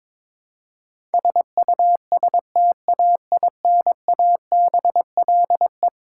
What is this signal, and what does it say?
Morse code, heard as a keyed tone, about 22 wpm, 700 Hz. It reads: SUSTAINABLE
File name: practice / 22wpm / Eleven Letter Words - 22wpm